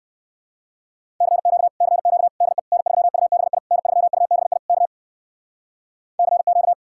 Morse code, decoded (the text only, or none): CQ CQ DE N3UBI N3UBI K CQ